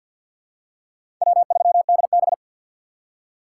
Morse code, heard as a keyed tone, about 47 wpm, 690 Hz. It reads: W3DB
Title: USA Call Signs - 6x Character Spacing - 50wpm